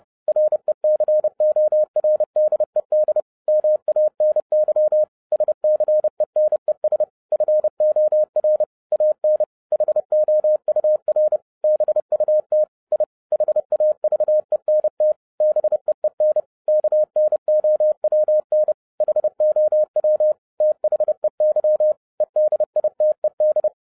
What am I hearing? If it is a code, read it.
Morse code, heard as a keyed tone, about 30 wpm, 610 Hz. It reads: RECORDED MANY SCENES FOR AN HOUR BUT I HAVENT BEEN KNOWN HOW THEY EDITED